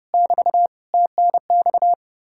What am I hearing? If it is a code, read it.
Morse code, heard as a keyed tone, about 36 wpm, 700 Hz. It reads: <BT> TNX
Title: Speed-Racing Common QSO Phrases - 30wpm